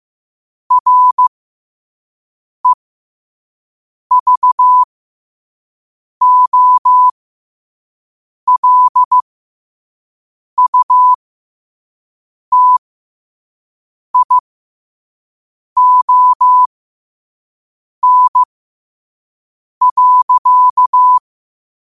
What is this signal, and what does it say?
Morse code, heard as a keyed tone, about 15 wpm, spaced as for 5 wpm, 1000 Hz. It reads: REVOLUTION.
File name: qotdfiles / 5wpm